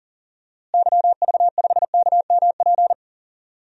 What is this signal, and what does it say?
Morse code, heard as a keyed tone, about 37 wpm, 700 Hz. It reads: YV5KMP
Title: DXCC-100 Call Signs - 6 Characters - Repeated 2x - 40wpm